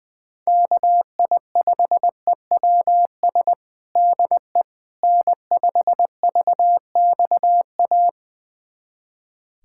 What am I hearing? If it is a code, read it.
Morse code, heard as a keyed tone, about 20 wpm, 700 Hz. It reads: KI5EWS DE N5VXA
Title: USA Callsign DE Callsign 20wpm